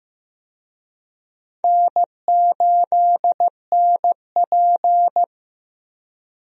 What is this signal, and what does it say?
Morse code, heard as a keyed tone, about 15 wpm, 700 Hz. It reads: N8NP
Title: USA Call Signs - 4 Characters - Repeated 2x - 15wpm